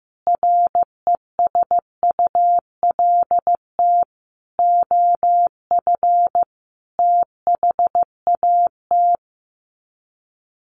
Morse code, heard as a keyed tone, about 15 wpm, 700 Hz. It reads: RESULT OF THAT
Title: Sets of 5 Words Encore 15wpm